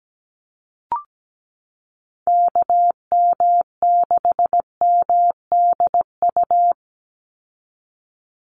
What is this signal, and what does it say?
Morse code, heard as a keyed tone, about 17 wpm, 700 Hz. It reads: KM6MDU
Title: USA Call Signs 17wpm